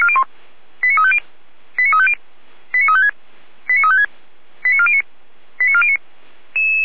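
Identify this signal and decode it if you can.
Five-tone selective call (ZVEI), heard as quick runs of five tones, tones of about 70 ms. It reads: 89350 89350 89356 89356 89409 89409
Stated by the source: ZVEI five-tone selective call 89104 received by SDR, from the Feuerwehr Orlamünde (fire brigade) audio server.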